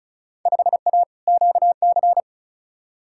Morse code, heard as a keyed tone, about 36 wpm, 700 Hz. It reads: 5A QC
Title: Field Day Exchanges Encore - 35wpm